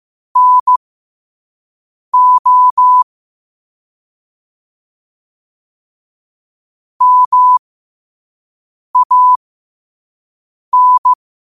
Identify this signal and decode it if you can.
Morse code, heard as a keyed tone, about 15 wpm, spaced as for 5 wpm, 1000 Hz. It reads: NO MAN
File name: qotdfiles / 5wpm